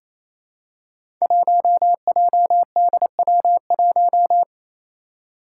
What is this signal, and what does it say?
Morse code, heard as a keyed tone, about 28 wpm, 700 Hz. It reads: 1JDW1
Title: Five Challenging Letter-Number Combinations 28wpm